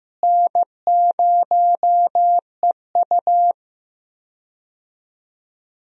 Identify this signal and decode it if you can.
Morse code, heard as a keyed tone, about 16 wpm, 700 Hz. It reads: N0EU
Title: USA Call Signs - 4 Characters - Repeated 3x - 15wpm